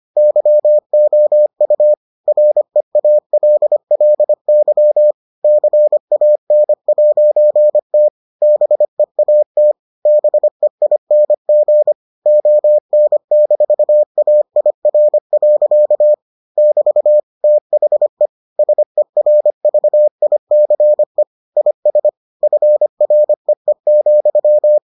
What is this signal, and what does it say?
Morse code, heard as a keyed tone, about 25 wpm, 600 Hz. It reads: YOU REALLY CAN'T BEAT BEING ON-AIR. <BT> THE SERVICE IS FREE,